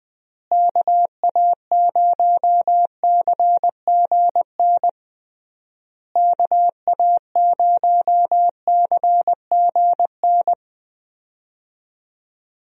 Morse code, heard as a keyed tone, about 20 wpm, 700 Hz. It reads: KA0CGN KA0CGN
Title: USA Call Signs - 6 Characters - Repeated 3x - 20wpm